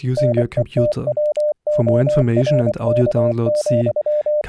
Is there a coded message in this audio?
Yes: Morse code from 0.2 to 4.4 s, heard as a keyed tone, about 25 wpm, 600 Hz, with speech. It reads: NETWORK,